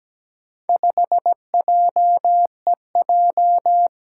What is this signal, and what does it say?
Morse code, heard as a keyed tone, about 17 wpm, 700 Hz. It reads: 5JEJ
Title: Four Challenging Letter-Number Combinations 17wpm